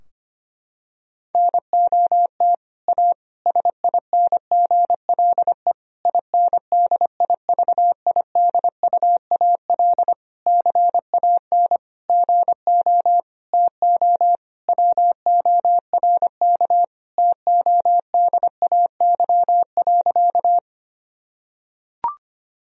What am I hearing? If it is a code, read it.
Morse code, heard as a keyed tone, about 25 wpm, 700 Hz. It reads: NOT A SINGLE INDIVIDUAL CAN GO TO WORK TODAY.
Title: Sentences from Top 1000 Words - Review Part 4 - 25wpm